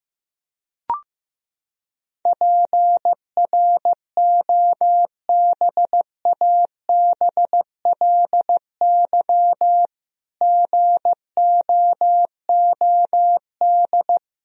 Morse code, heard as a keyed tone, about 15 wpm, 700 Hz. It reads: PROBABLY GOOD
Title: Sentences from Top 1000 Words - Review Part 4 - 15wpm